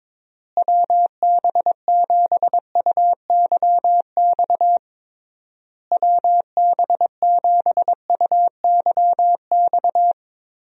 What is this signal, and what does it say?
Morse code, heard as a keyed tone, about 22 wpm, 700 Hz. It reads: WB7UYX WB7UYX
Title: USA Call Signs - 6 Characters - Repeated 3x - 22wpm